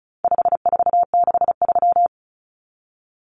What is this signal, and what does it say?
Morse code, heard as a keyed tone, about 34 wpm, 700 Hz. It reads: 5463
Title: Numbers - 4-Digit 35wpm